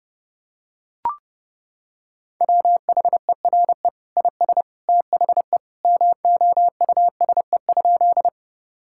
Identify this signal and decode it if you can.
Morse code, heard as a keyed tone, about 30 wpm, 700 Hz. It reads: WHERE IS THE MOUSE?
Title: Sentences from Top 1000 Words - Review Part 5 - 30wpm